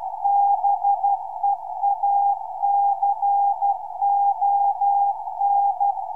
Morse code, heard as a keyed tone, about 12 wpm, 800 Hz. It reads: BEACON